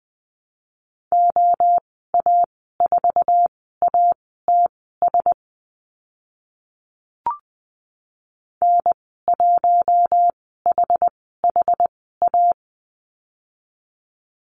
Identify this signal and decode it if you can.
Morse code, heard as a keyed tone, about 20 wpm, spaced as for 15 wpm, 700 Hz. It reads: OA4ATS N1HHA
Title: 069 Lesson 12g - Letter d - Call Signs 20wpm